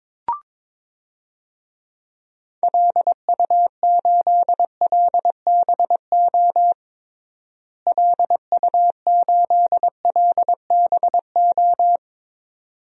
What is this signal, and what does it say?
Morse code, heard as a keyed tone, about 22 wpm, 700 Hz. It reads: LU8LBO LU8LBO
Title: DXCC-100 Call Signs - 6 Characters - Repeated 2x - 22wpm